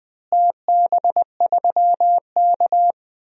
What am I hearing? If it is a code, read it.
Morse code, heard as a keyed tone, about 20 wpm, 700 Hz. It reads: TB3K
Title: DXCC-100 Call Signs - 4 Characters - Repeated 3x - 20wpm